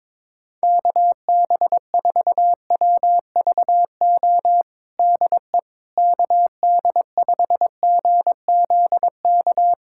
Morse code, heard as a keyed tone, about 22 wpm, 700 Hz. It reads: KB4WVO DE KD5GZK
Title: USA Callsign DE Callsign 22wpm